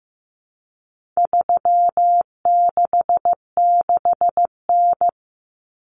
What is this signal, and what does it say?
Morse code, heard as a keyed tone, about 15 wpm, 700 Hz. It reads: 366N
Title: Common Cut Numbers - 4-Digit 15wpm